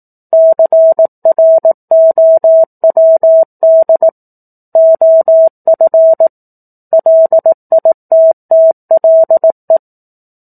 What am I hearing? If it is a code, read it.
Morse code, heard as a keyed tone, about 18 wpm, 650 Hz. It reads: CROWD OF LITTLE
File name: 04-Chapter-04 18 WPM